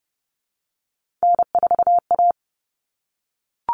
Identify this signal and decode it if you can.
Morse code, heard as a keyed tone, about 30 wpm, 700 Hz. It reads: N4A
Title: Common Cut Numbers - 3-Digit 30wpm